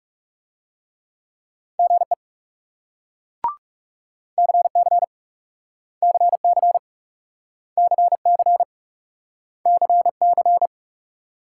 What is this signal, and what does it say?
Morse code, heard as a keyed tone, about 38 wpm, 700 Hz. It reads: GE CC CC CC CC